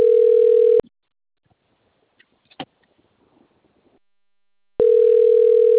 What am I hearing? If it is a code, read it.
Ringback tone (North America), 440 + 480 Hz.